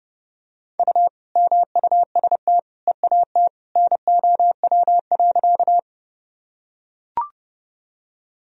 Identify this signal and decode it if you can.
Morse code, heard as a keyed tone, about 30 wpm, 700 Hz. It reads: U MUST EAT NOW.
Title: Sentences from Top 300 Words and Abbreviations 30wpm